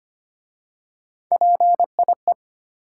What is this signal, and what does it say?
Morse code, heard as a keyed tone, about 25 wpm, 700 Hz. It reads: PIE
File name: practice / 25wpm / Difficult Words with Hints - 25wpm